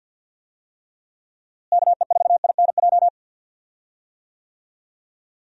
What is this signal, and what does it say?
Morse code, heard as a keyed tone, about 50 wpm, 700 Hz. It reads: KE4INJ
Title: USA Call Signs - 6 Characters - Repeated 3x - 50wpm